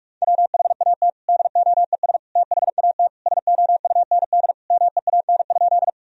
Morse code, heard as a keyed tone, about 46 wpm, 700 Hz. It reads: WHAT DOES THAT SOUND MEAN?
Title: Sentences from Top 1000 Words - Review Part 1 - 45wpm